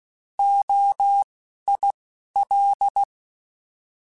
Morse code, heard as a keyed tone, about 16 wpm, spaced as for 11 wpm, 800 Hz. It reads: OIL
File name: Alphabetical / Alphabetical mp3s / Os @10